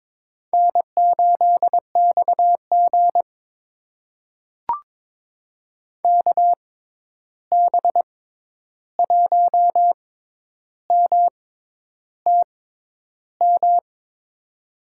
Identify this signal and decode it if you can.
Morse code, heard as a keyed tone, about 22 wpm, spaced as for 10 wpm, 700 Hz. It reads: N8XG KB1MTM
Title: USA Call Signs - 6x Character Spacing - 22wpm